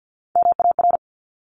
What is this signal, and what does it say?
Morse code, heard as a keyed tone, about 50 wpm, 700 Hz. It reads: MAR
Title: Top Three Letter Combinations in Words 50wpm